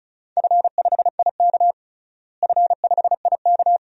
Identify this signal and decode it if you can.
Morse code, heard as a keyed tone, about 34 wpm, 700 Hz. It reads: F5IK F5IK
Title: DXCC-100 Call Signs - 4 Characters - Repeated 2x - 35wpm